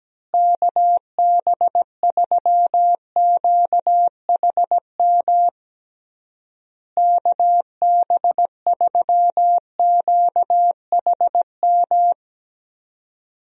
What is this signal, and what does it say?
Morse code, heard as a keyed tone, about 17 wpm, 700 Hz. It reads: KB3QHM KB3QHM
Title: USA Call Signs - 6 Characters - Repeated 3x - 17wpm